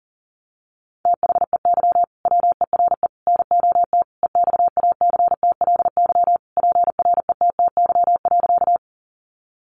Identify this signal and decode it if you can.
Morse code, heard as a keyed tone, about 40 wpm, 700 Hz. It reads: THEY WERE NOT EXACTLY PRETTY.
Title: Sentences from Top 1000 Words 40wpm